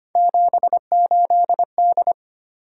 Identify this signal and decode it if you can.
Morse code, heard as a keyed tone, about 25 wpm, 700 Hz. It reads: 78D
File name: Three Letter-Number Combinations - 25wpm